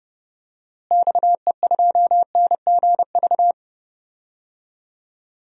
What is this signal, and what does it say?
Morse code, heard as a keyed tone, about 30 wpm, 700 Hz. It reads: XE2NGV
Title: DXCC-100 Call Signs - 6 Characters - Repeated 3x - 30wpm